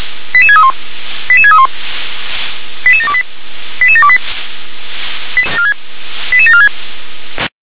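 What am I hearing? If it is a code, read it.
Five-tone selective call (ZVEI), heard as quick runs of five tones, tones of about 70 ms. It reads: 88631 88631 88728 88728 88746
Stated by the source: Five-tone ZVEI selective call received by SDR.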